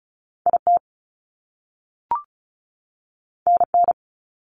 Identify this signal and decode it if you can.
Morse code, heard as a keyed tone, about 36 wpm, 700 Hz. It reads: IT NN
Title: Top Two Letter Combinations in Words 35wpm